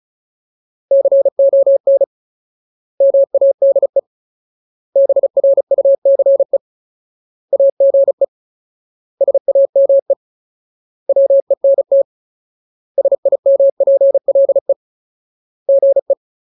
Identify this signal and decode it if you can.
Morse code, heard as a keyed tone, about 34 wpm, 550 Hz. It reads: CON MADE BRUCE AGE SAME WENT SIMPLE GE
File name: Ultimate Word List / 35WPM Ultimate